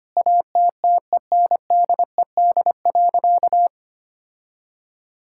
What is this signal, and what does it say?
Morse code, heard as a keyed tone, about 25 wpm, 700 Hz. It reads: ATTENDED.
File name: practice / 25wpm / Sentences with 4-Word Phrases - 25wpm